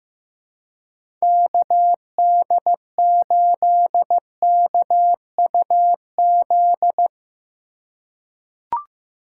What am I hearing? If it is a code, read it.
Morse code, heard as a keyed tone, about 15 wpm, 700 Hz. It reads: KD8KUZ
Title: USA Call Signs - 6 Characters - 4x Character Spacing - 15wpm